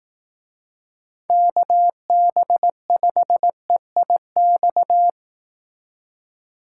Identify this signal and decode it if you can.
Morse code, heard as a keyed tone, about 19 wpm, 700 Hz. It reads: KB5EIX
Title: Speed Racing - USA 2x3 Call Signs 15wpm